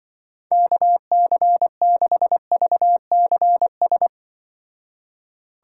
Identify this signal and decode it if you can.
Morse code, heard as a keyed tone, about 25 wpm, 700 Hz. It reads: KC6VCS